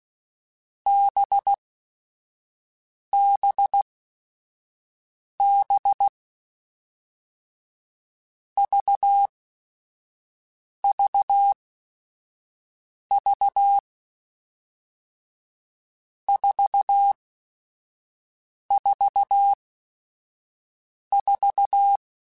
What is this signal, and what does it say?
Morse code, heard as a keyed tone, about 16 wpm, spaced as for 5 wpm, 800 Hz. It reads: BBB VVV 444
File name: Groups@5 / Groups@5 mp3s / VB46@5